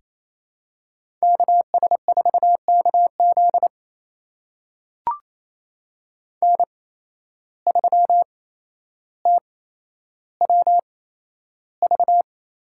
Morse code, heard as a keyed tone, about 28 wpm, spaced as for 12 wpm, 700 Hz. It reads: KS4KZ N3TWV
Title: USA Call Signs - 5 Characters - 8x Character Spacing - 28wpm